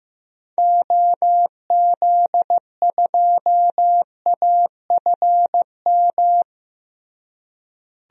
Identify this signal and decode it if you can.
Morse code, heard as a keyed tone, about 15 wpm, 700 Hz. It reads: OZ2AFM
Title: DXCC-100 Call Signs - 6 Characters - Repeated 2x - 15wpm